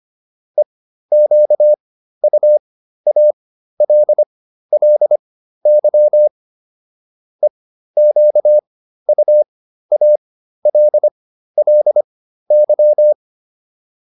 Morse code, heard as a keyed tone, about 25 wpm, spaced as for 13 wpm, 600 Hz. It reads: EQUALLY EQUALLY
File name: LY7 13